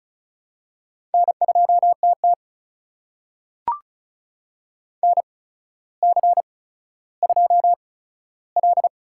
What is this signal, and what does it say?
Morse code, heard as a keyed tone, about 36 wpm, spaced as for 15 wpm, 700 Hz. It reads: N2TT NC2L
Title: USA Call Signs - 4 Characters - 8x Character Spacing - 35wpm